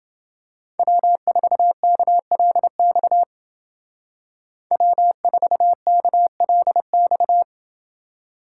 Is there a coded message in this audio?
Yes: Morse code from 0.8 to 7.4 s, heard as a keyed tone, about 28 wpm, 700 Hz. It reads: W4KLX W4KLX